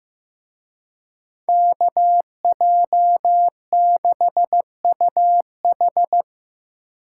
Morse code, heard as a keyed tone, about 17 wpm, 700 Hz. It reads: KJ6UH
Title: Speed Racing - USA 2x2 Call Signs 15wpm